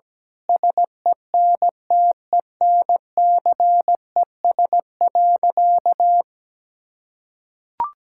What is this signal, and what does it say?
Morse code, heard as a keyed tone, about 17 wpm, 700 Hz. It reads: SENTENCES.
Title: Pangram Challenge v2 - 17wpm